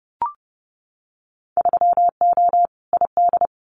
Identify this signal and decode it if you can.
Morse code, heard as a keyed tone, about 32 wpm, 700 Hz. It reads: 3O ID
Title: Winter Field Day Exchanges 30wpm